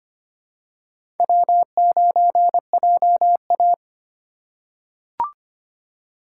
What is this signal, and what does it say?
Morse code, heard as a keyed tone, about 25 wpm, 700 Hz. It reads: W9JA
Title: USA Call Signs - 4 Characters - 3x Character Spacing - 25wpm